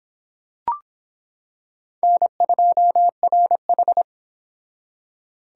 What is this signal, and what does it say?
Morse code, heard as a keyed tone, about 26 wpm, 700 Hz. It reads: N2RH